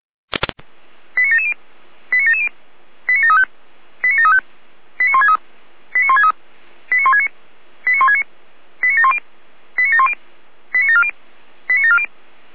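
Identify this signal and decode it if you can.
Five-tone selective call (ZVEI), heard as quick runs of five tones, tones of about 70 ms. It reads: 89770 89770 89735 89735 89163 89163 89169 89169 89710 89710 89740 89740
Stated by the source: ZVEI five-tone selective call received on an SDR.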